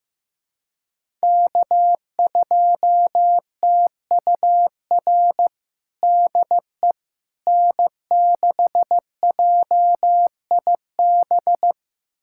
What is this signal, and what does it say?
Morse code, heard as a keyed tone, about 15 wpm, 700 Hz. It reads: K2TUR DE N6JIB
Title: USA Callsign DE Callsign - 5 Characters - 15wpm